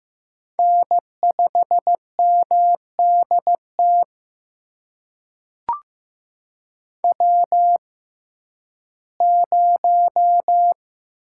Morse code, heard as a keyed tone, about 15 wpm, spaced as for 11 wpm, 700 Hz. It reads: N5MDT W0
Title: USA Call Signs - 6x Character Spacing - 15wpm